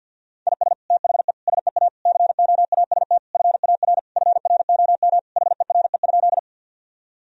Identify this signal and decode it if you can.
Morse code, heard as a keyed tone, about 51 wpm, 700 Hz. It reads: IS THE SEA COAST FAR FROM HERE?
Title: Sentences from Top 1000 Words - Review Part 4 - 50wpm